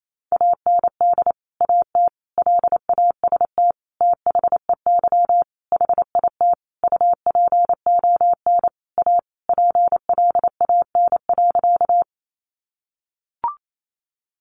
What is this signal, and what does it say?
Morse code, heard as a keyed tone, about 28 wpm, 700 Hz. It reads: AND AT LAST THEY HIT UPON A PLAN.
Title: Sentences from Top 1200 Words 28wpm